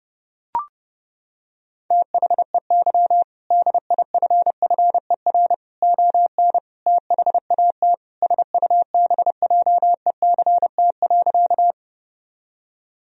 Morse code, heard as a keyed tone, about 30 wpm, 700 Hz. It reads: THEY DIFFER ON THAT SUBJECT.
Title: Sentences from Top 1000 Words - Review Part 1 - 30wpm